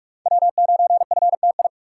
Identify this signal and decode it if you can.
Morse code, heard as a keyed tone, about 45 wpm, 700 Hz. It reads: W9FTI
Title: USA Call Signs - 5 Characters - 3x Character Spacing - 45wpm